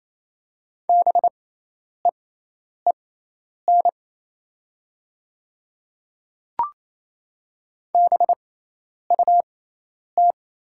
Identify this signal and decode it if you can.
Morse code, heard as a keyed tone, about 28 wpm, spaced as for 10 wpm, 700 Hz. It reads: BEEN BUT
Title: 6x Character Spacing - Top 100 Words - 28wpm